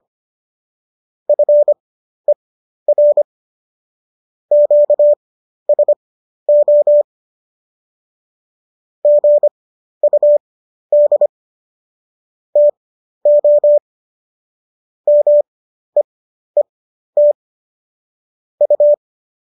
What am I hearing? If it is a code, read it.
Morse code, heard as a keyed tone, about 25 wpm, spaced as for 12 wpm, 590 Hz. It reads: FER QSO GUD TO MEET U
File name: qso012 12